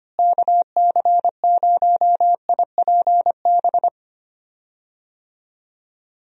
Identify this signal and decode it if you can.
Morse code, heard as a keyed tone, about 25 wpm, 700 Hz. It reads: KC0IPB
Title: USA Call Signs - 6 Characters - Repeated 2x - 25wpm